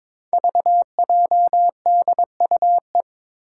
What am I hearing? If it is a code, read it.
Morse code, heard as a keyed tone, about 21 wpm, 700 Hz. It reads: VJDUE
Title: Five Challenging Letter-Number Combinations 22wpm